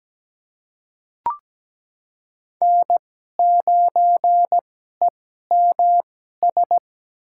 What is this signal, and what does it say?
Morse code, heard as a keyed tone, about 17 wpm, spaced as for 12 wpm, 700 Hz. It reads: N9EMS
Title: USA Call Signs - 5 Characters - 2x Character Spacing - 17wpm